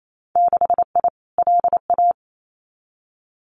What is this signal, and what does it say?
Morse code, heard as a keyed tone, about 28 wpm, 700 Hz. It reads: 6I LA
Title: Winter Field Day Exchanges Encore 28wpm